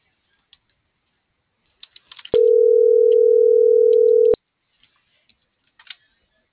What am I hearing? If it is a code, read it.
Ringback tone (North America), 440 + 480 Hz.